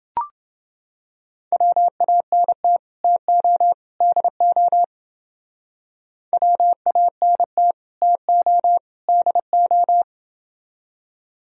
Morse code, heard as a keyed tone, about 29 wpm, 700 Hz. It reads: WANT TO DO WANT TO DO